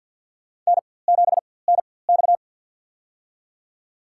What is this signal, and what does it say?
Morse code, heard as a keyed tone, about 53 wpm, spaced as for 24 wpm, 700 Hz. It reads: N7NX
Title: USA Call Signs - 4x Character Spacing - 50wpm